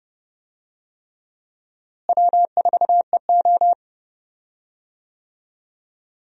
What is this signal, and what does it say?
Morse code, heard as a keyed tone, about 30 wpm, 700 Hz. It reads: W4EO
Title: USA Call Signs - 4 Characters - Repeated 3x - 30wpm